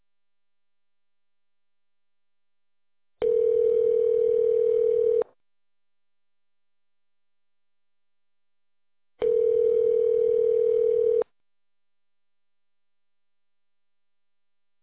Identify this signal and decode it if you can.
Ringback tone (North America), 440 + 480 Hz.